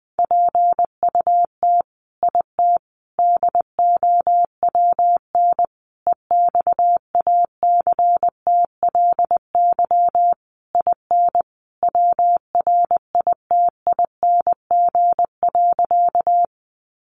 Morse code, heard as a keyed tone, about 20 wpm, 700 Hz. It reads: PUT IT DOWN EXACTLY IN WRITING.
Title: Sentences from Top 900 Words 20wpm